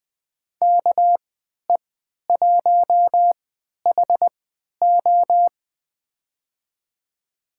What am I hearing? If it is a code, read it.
Morse code, heard as a keyed tone, about 20 wpm, spaced as for 11 wpm, 700 Hz. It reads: KE1HO
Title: USA Call Signs - 5 Characters - 3x Character Spacing - 20wpm